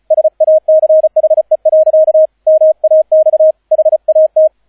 Morse code, heard as a keyed tone, about 36 wpm, 630 Hz, with radio noise. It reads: SACHE. MAX HAT